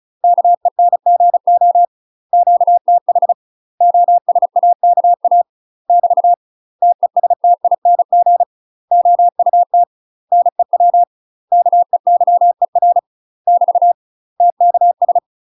Morse code, heard as a keyed tone, about 33 wpm, 700 Hz. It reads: KENGO QTH OSAKA <BT> TESTING OUT NEW KEYER <BT> TKS
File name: UK MASTER QSOs TRAINING / 35WPM / UK MASTER QSOs 0013